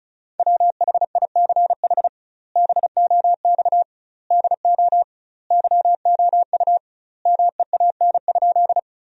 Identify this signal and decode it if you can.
Morse code, heard as a keyed tone, about 36 wpm, 700 Hz. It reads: WHICH BOX DO YOU MEAN?
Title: 2x Word-Spacing - Sentences from Top 500 Words - 35wpm